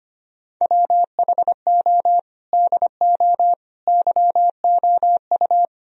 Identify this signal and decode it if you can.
Morse code, heard as a keyed tone, about 25 wpm, 700 Hz. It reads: WHO DO YOU
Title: Sentences from Top 1000 Words - Review Part 4 - 25wpm